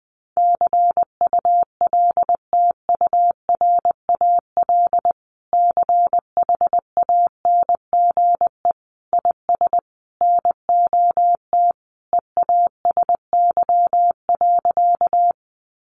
Morse code, heard as a keyed tone, about 20 wpm, 700 Hz. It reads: CULTURAL CHANGE IS NOT EASY.
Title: Sentences from Top 1500 Words 20wpm